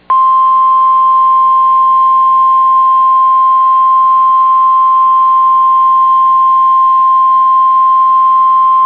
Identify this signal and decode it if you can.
Emergency Alert System (SAME) tones (the 1050 Hz attention tone). It is heard from 1.0 to 8.1 s.